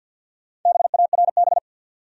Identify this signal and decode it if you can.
Morse code, heard as a keyed tone, about 50 wpm, 700 Hz. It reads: BARB
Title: USA Common Names - 50wpm